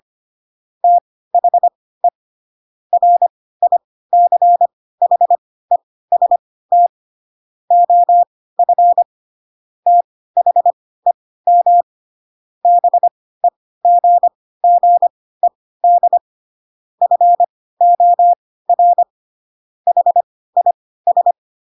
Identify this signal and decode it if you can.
Morse code, heard as a keyed tone, about 25 wpm, spaced as for 16 wpm, 700 Hz. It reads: THE RICHEST OF THEM BEGGED FOR HIS
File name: Pinocchio / Pinocchio ch3 10 25 ik1hge